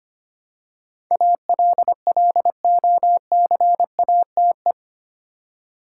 Morse code, heard as a keyed tone, about 25 wpm, 700 Hz. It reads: ALLOCATE
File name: Eight Letter Words - 25wpm